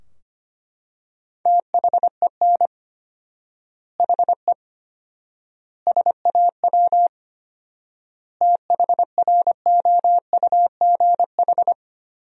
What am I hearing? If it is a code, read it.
Morse code, heard as a keyed tone, about 25 wpm, 700 Hz. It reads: THEN HE SAW THROUGH
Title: 4x Word-Spacing - Sentences from Top 400 Words - 25wpm